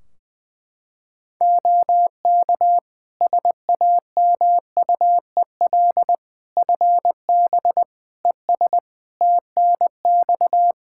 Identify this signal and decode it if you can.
Morse code, heard as a keyed tone, about 20 wpm, 700 Hz. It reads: OK SAMUEL FB ES TNX
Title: Practice QSO - Exchange 2 - Info - 20wpm